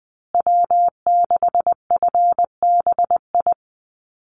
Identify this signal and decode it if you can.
Morse code, heard as a keyed tone, about 20 wpm, 700 Hz. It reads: W6FBI
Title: USA Call Signs Repeated 2x 20wpm